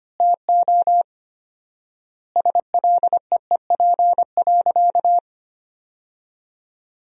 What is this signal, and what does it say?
Morse code, heard as a keyed tone, about 25 wpm, 700 Hz. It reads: TO SLEEP.
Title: 4x Word-Spacing - Sentences from Top 500 Words - 25wpm